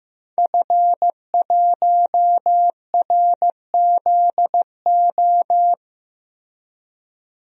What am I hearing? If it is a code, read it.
Morse code, heard as a keyed tone, about 15 wpm, 700 Hz. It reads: F1RZO
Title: DXCC-100 Call Signs - 5 Characters - Repeated 2x - 15wpm